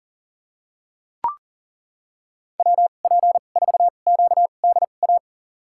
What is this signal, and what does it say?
Morse code, heard as a keyed tone, about 40 wpm, spaced as for 29 wpm, 700 Hz. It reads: WP4QDA
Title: USA Call Signs - 6 Characters - 2x Character Spacing - 40wpm